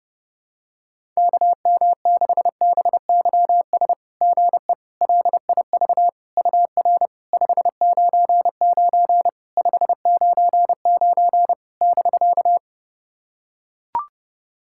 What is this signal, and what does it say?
Morse code, heard as a keyed tone, about 30 wpm, 700 Hz. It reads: KM6BYS GE LIV UR 599 599 <BK>
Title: SOTA QSO Practice 30wpm